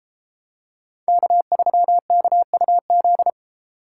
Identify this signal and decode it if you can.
Morse code, heard as a keyed tone, about 32 wpm, 700 Hz. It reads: K3KUZ